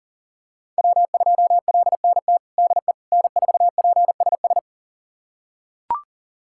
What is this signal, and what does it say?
Morse code, heard as a keyed tone, about 40 wpm, 700 Hz. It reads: W2LNT DE N4PSS
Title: USA Callsign DE Callsign - 5 Characters - 40wpm